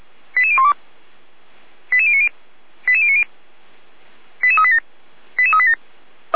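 Five-tone selective call (ZVEI), heard as quick runs of five tones, tones of about 70 ms. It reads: 88012 88090 88090 88387 88387
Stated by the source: Five-tone ZVEI selective call 88149 received by SDR, logged by fire brigade Feuerwehr Orlamünde.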